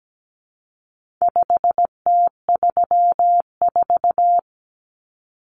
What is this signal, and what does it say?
Morse code, heard as a keyed tone, about 17 wpm, 700 Hz. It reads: 5T34
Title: Common Cut Numbers - 4-Digit 17wpm